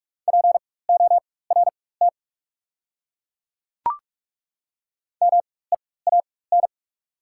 Morse code, heard as a keyed tone, about 43 wpm, spaced as for 21 wpm, 690 Hz. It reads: PORT MEAN
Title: 4x Character Spacing - Top 200 Words - 45wpm